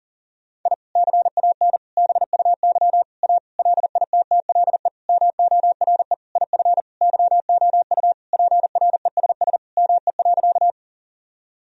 Morse code, heard as a keyed tone, about 40 wpm, 700 Hz. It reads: I CAN BUY A LITTLE MORE IF YOU PRESS ME.
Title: Sentences from Top 1000 Words - Review Part 1 - 40wpm